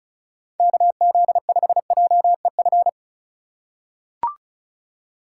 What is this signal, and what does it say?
Morse code, heard as a keyed tone, about 36 wpm, 700 Hz. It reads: KZ5JEF
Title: USA Call Signs - 6 Characters - 3x Character Spacing - 35wpm